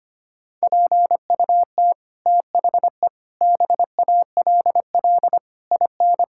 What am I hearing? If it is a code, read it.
Morse code, heard as a keyed tone, about 25 wpm, 700 Hz. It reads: PUT THE BALL IN
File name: Sets of 9 Words - 25wpm